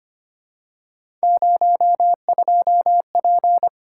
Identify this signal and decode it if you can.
Morse code, heard as a keyed tone, about 25 wpm, 700 Hz. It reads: 02P
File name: Three Letter-Number Combinations - 25wpm